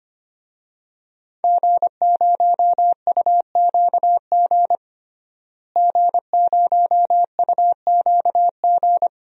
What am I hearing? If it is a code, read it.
Morse code, heard as a keyed tone, about 25 wpm, 700 Hz. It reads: G0UQG G0UQG
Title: DXCC-100 Call Signs - 5 Characters - Repeated 2x - 25wpm